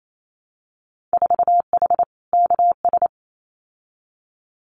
Morse code, heard as a keyed tone, about 28 wpm, 700 Hz. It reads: 4H KS
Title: Winter Field Day Exchanges Encore 28wpm